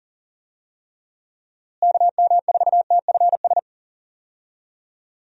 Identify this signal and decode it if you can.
Morse code, heard as a keyed tone, about 40 wpm, 700 Hz. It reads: KM4TFS
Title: USA Call Signs - 6 Characters - Repeated 3x - 40wpm